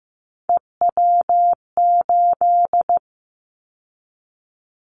Morse code, heard as a keyed tone, about 15 wpm, 700 Hz. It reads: EW8
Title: Callsign Prefixes for Top 50 DXCC - 15wpm